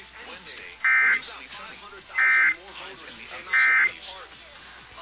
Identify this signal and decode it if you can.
Emergency Alert System (SAME) tones, heard as harsh buzzing bursts, with speech. It is heard from 0.7 to 4.0 s.